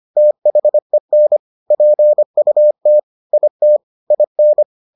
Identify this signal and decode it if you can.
Morse code, heard as a keyed tone, about 25 wpm, 600 Hz. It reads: THEN PUT IT IN